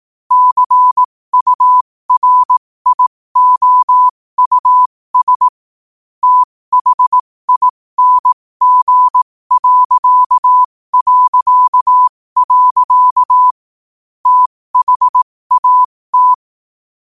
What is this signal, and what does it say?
Morse code, heard as a keyed tone, about 18 wpm, spaced as for 15 wpm, 1000 Hz. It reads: CURIOUS THING... THAT
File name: qotdfiles / 15wpm